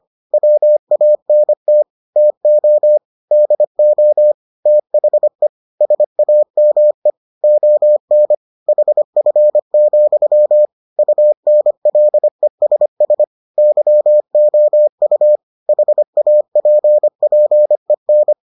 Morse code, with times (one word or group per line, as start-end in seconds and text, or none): 0.3-1.8 WANT
2.2-3.0 TO
3.3-4.3 DO
4.7-5.5 THE
5.8-7.1 SAME
7.4-8.3 ON
8.7-10.7 HF,
11.0-13.2 UNLESS
13.6-15.4 YOU
15.7-18.3 HAPPEN